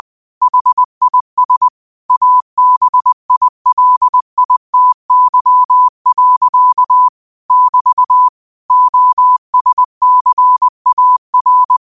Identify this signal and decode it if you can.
Morse code, heard as a keyed tone, about 20 wpm, 1000 Hz. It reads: HIS ABILITY. <BT> OSCAR
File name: qotdfiles / 20wpm